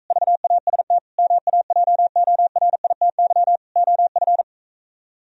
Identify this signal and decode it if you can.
Morse code, heard as a keyed tone, about 45 wpm, 700 Hz. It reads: VAST MAJORITY OF